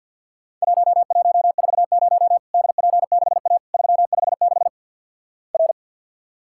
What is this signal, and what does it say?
Morse code, heard as a keyed tone, about 50 wpm, 700 Hz. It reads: 1140 DP6A 356 R
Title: Worked All Europe DX Contest - QTC Exchange with Repeats - 50wpm